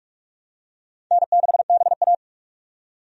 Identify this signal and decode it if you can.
Morse code, heard as a keyed tone, about 47 wpm, 700 Hz. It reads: N6BA
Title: USA Call Signs - 4 Characters - 45wpm